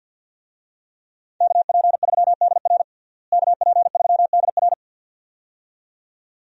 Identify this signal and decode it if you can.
Morse code, heard as a keyed tone, about 51 wpm, 700 Hz. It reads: KP3DR KP3DR
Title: USA Call Signs - 5 Characters - Repeated 2x - 50wpm